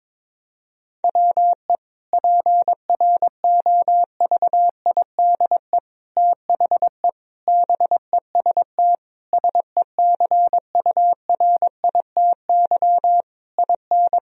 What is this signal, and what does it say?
Morse code, heard as a keyed tone, about 22 wpm, 700 Hz. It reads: WE PROVIDE THE BEST SECURITY IN
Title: Sentences from Top 1000 Words - Review Part 4 - 22wpm